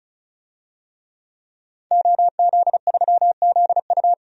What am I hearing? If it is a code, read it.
Morse code, heard as a keyed tone, about 34 wpm, 700 Hz. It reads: OZ3ZU
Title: DXCC-100 Call Signs - 5 Characters - 35wpm